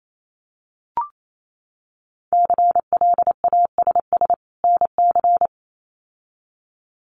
Morse code, heard as a keyed tone, about 28 wpm, 700 Hz. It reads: CLASS NC
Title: Winter Field Day Exchanges 28wpm